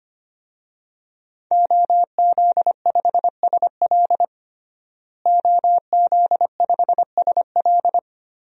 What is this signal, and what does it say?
Morse code, heard as a keyed tone, about 25 wpm, 700 Hz. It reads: OZ5SL OZ5SL
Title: DXCC-100 Call Signs - 5 Characters - Repeated 2x - 25wpm